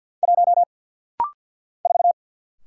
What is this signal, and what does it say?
Morse code, heard as a keyed tone, about 50 wpm, 700 Hz. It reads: 1 4
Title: Single Letter-Number Alt - Rapid-Fire 50wpm